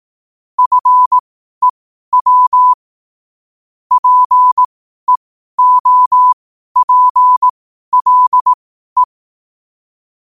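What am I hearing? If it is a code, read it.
Morse code, heard as a keyed tone, about 18 wpm, spaced as for 12 wpm, 1000 Hz. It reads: FEW PEOPLE
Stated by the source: 12 wpm Morse code.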